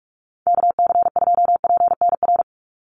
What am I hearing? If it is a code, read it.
Morse code, heard as a keyed tone, about 44 wpm, 700 Hz. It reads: KC2PNR
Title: USA Call Signs 45wpm